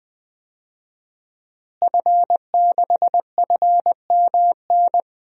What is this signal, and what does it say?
Morse code, heard as a keyed tone, about 20 wpm, 700 Hz. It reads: F6FMN
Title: DXCC-100 Call Signs - 5 Characters - Repeated 3x - 20wpm